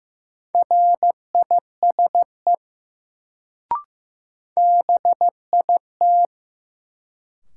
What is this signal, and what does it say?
Morse code, heard as a keyed tone, about 15 wpm, 700 Hz. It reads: RISE BIT
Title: Sentences from Top 900 Words - New Word Drill - 15wpm